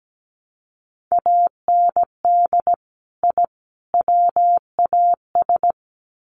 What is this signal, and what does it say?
Morse code, heard as a keyed tone, about 17 wpm, 700 Hz. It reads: AND I WAS
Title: Sets of 3 Words 17wpm